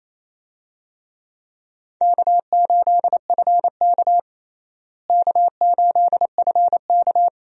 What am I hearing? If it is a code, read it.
Morse code, heard as a keyed tone, about 28 wpm, 700 Hz. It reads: K8FK K8FK
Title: USA Call Signs - 4 Characters - Repeated 2x - 28wpm